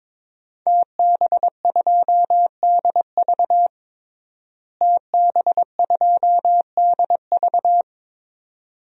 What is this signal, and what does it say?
Morse code, heard as a keyed tone, about 22 wpm, 700 Hz. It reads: TB2DV TB2DV
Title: DXCC-100 Call Signs - 5 Characters - Repeated 2x - 22wpm